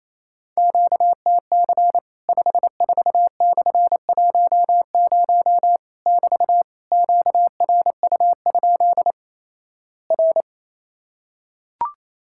Morse code, heard as a keyed tone, about 28 wpm, 700 Hz. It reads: QTC 54/10 <BT> QRU? R
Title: Worked All Europe DX Contest - QTC Exchange - 28wpm